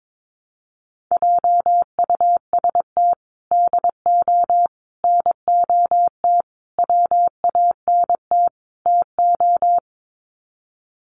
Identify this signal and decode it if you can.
Morse code, heard as a keyed tone, about 22 wpm, 700 Hz. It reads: JUST DO NOT WANT TO
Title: Sets of 5 Words Encore 22wpm